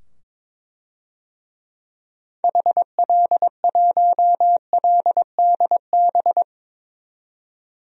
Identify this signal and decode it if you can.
Morse code, heard as a keyed tone, about 22 wpm, 700 Hz. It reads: HL1LDB
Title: DXCC-100 Call Signs - 6 Characters - 22wpm